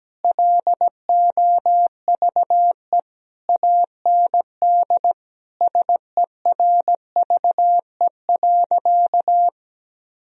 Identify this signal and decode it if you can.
Morse code, heard as a keyed tone, about 17 wpm, 700 Hz. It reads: LOVE AND SERVE.
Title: Sentences from Top 1000 Words - Review Part 2 - 17wpm